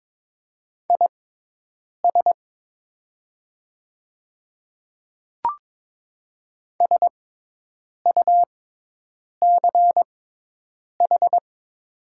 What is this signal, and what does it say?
Morse code, heard as a keyed tone, about 22 wpm, spaced as for 8 wpm, 700 Hz. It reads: IS SUCH
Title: 6x Character Spacing - Top 200 Words - 22wpm